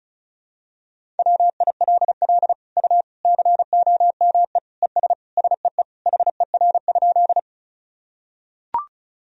Morse code, heard as a keyed tone, about 34 wpm, 700 Hz. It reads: WILL U COME ES SEE HER?
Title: Sentences from Top 100 Words and Abbreviations 35wpm